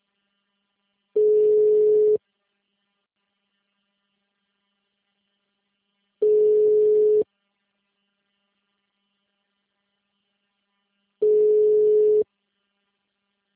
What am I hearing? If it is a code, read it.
Ringback tone (Europe, 425 Hz).